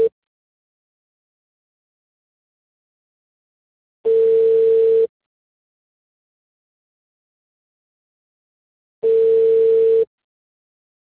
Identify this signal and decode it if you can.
Ringback tone (China, 450 Hz).